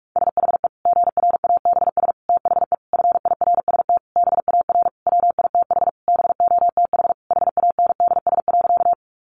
Morse code, heard as a keyed tone, about 44 wpm, 700 Hz. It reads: SHE GRABS THE FIRST BAR WITH BOTH HANDS.
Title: Easy Phrases 45wpm